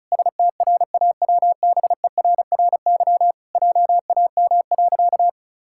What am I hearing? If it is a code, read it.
Morse code, heard as a keyed tone, about 34 wpm, 700 Hz. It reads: STRAWBERRY JAM.
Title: Pangram Challenge v2 - 35wpm